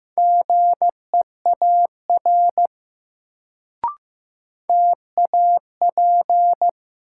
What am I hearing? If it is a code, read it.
Morse code, heard as a keyed tone, about 15 wpm, 700 Hz. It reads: GEAR TAP
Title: Sentences from Top 1400 Words - New Word Drill Alt - 15wpm